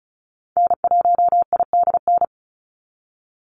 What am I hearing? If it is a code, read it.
Morse code, heard as a keyed tone, about 34 wpm, 700 Hz. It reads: N1IDN
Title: NA SOTA Activator Callsigns 35wpm